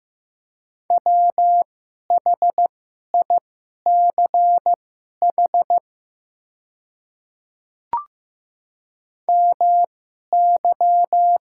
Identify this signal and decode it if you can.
Morse code, heard as a keyed tone, about 15 wpm, spaced as for 11 wpm, 700 Hz. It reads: WHICH MY
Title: 2x Character Spacing - Top 100 Words - 15wpm